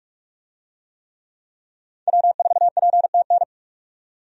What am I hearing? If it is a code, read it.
Morse code, heard as a keyed tone, about 47 wpm, 700 Hz. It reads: W4PTN
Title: USA Call Signs - 5 Characters - 45wpm